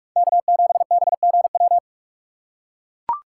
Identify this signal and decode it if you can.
Morse code, heard as a keyed tone, about 44 wpm, 700 Hz. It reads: K7BGW